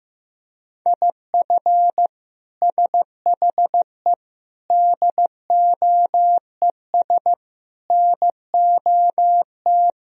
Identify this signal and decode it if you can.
Morse code, heard as a keyed tone, about 15 wpm, 700 Hz. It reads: IF SHE DOES NOT
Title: Sentences from Top 1000 Words - Review Part 3 - 15wpm